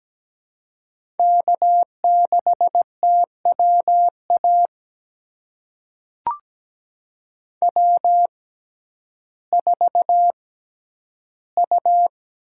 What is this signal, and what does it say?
Morse code, heard as a keyed tone, about 17 wpm, spaced as for 10 wpm, 700 Hz. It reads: K6TWA W4U
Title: USA Call Signs - 5 Characters - 6x Character Spacing - 17wpm